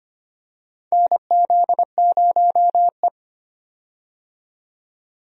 Morse code, heard as a keyed tone, about 25 wpm, 700 Hz. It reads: NZ0E
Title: USA Call Signs - 4 Characters - 25wpm